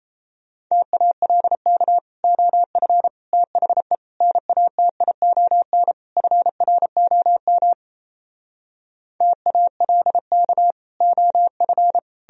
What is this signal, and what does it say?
Morse code, heard as a keyed tone, about 37 wpm, 700 Hz. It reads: TALK OF THE NATION FROM TALK OF